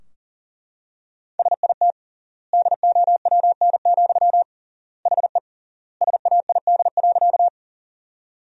Morse code, heard as a keyed tone, about 40 wpm, 700 Hz. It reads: SIT DOWN, HE SAID.
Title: 3x Word-Spacing - Sentences from Top 500 Words - 40wpm